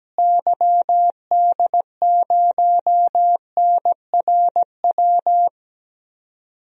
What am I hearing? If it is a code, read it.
Morse code, heard as a keyed tone, about 17 wpm, 700 Hz. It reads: YD0NRW
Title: DXCC-100 Call Signs - 6 Characters - Repeated 3x - 17wpm